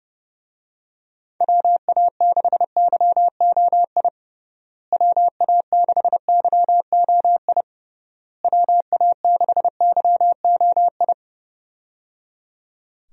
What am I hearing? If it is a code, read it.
Morse code, heard as a keyed tone, about 30 wpm, 700 Hz. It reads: WA6YOI WA6YOI WA6YOI
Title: USA Call Signs - 6 Characters - Repeated 3x - 30wpm